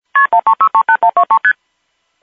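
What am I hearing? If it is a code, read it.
Multi-frequency trunk signalling, heard as fast pairs of tones, tones of about 90 ms. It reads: KP13638123ST
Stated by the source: MF tones on a North American local telephone company automated intercept recording, from an Automated Intercept System (AIS) that announces the dialed number and why the call failed.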